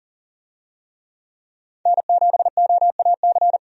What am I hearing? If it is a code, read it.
Morse code, heard as a keyed tone, about 40 wpm, 700 Hz. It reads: N7OAC
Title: USA Call Signs - 5 Characters - 40wpm